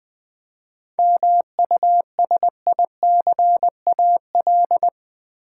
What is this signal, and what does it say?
Morse code, heard as a keyed tone, about 20 wpm, 700 Hz. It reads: MUSICAL
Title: 7 Letter Words - First 3 Letters Given - 20wpm